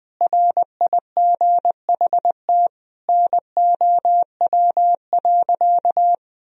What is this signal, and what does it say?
Morse code, heard as a keyed tone, about 20 wpm, 700 Hz. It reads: RIGHT NOW.
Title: Sentences from Top 1000 Words - Review Part 4 - 20wpm